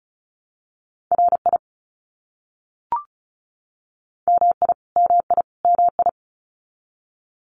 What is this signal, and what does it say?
Morse code, heard as a keyed tone, about 34 wpm, 700 Hz. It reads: RI MI MI MI
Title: US States Repeated 3x 35wpm